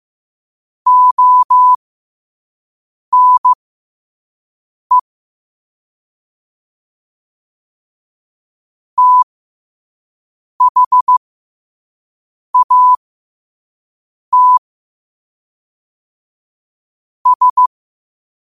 Morse code, heard as a keyed tone, about 15 wpm, spaced as for 5 wpm, 1000 Hz. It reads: ONE THAT S
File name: qotdfiles / 5wpm